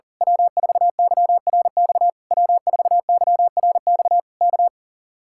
Morse code, heard as a keyed tone, about 35 wpm, 700 Hz. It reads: W4YRX W4YRX K
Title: Speed-Racing Common QSO Phrases Encore - 40wpm